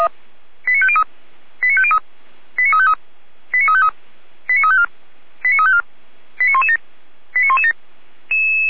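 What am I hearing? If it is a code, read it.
Five-tone selective call (ZVEI), heard as quick runs of five tones, tones of about 70 ms. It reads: 89503 89503 89353 89353 89354 89354 89117 89117
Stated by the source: ZVEI five-tone call received by SDR, recorded for Feuerwehr Orlamünde.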